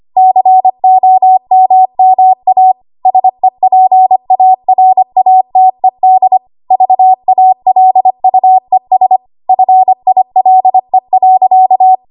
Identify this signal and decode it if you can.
Morse code, heard as a keyed tone, about 25 wpm, 750 Hz. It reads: COMMA SEPARATED VALUES FILE.
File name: Archive / 25 WPM / 25WPM